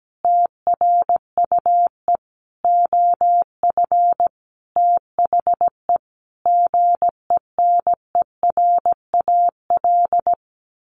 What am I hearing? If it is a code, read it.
Morse code, heard as a keyed tone, about 17 wpm, 700 Hz. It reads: TRUE OF THE GENERAL
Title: Sentences from Top 1400 Words 17wpm